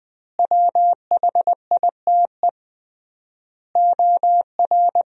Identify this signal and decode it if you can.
Morse code, heard as a keyed tone, about 20 wpm, 700 Hz. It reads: WHITE OR
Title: 3x Word-Spacing - Sentences from Top 400 Words - 20wpm